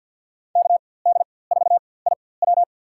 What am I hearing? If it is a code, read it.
Morse code, heard as a keyed tone, about 46 wpm, spaced as for 23 wpm, 700 Hz. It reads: KD4IW
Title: USA Call Signs - 5 Characters - 4x Character Spacing - 50wpm